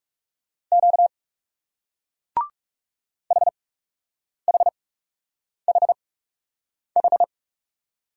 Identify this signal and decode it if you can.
Morse code, heard as a keyed tone, about 42 wpm, spaced as for 8 wpm, 700 Hz. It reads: Q HHHH